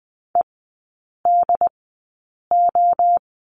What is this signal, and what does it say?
Morse code, heard as a keyed tone, about 20 wpm, spaced as for 8 wpm, 700 Hz. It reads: EDO
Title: 067 Lesson 12e - Letter d - Three Characters 20wpm